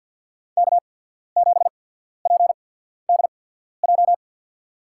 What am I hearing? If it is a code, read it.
Morse code, heard as a keyed tone, about 53 wpm, spaced as for 14 wpm, 700 Hz. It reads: K7PDJ
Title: USA Call Signs - 5 Characters - 8x Character Spacing - 50wpm